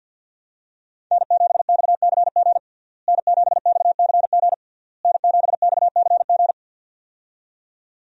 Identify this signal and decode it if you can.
Morse code, heard as a keyed tone, about 47 wpm, 700 Hz. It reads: N7XCG N7XCG N7XCG
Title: USA Call Signs - 5 Characters - Repeated 3x - 50wpm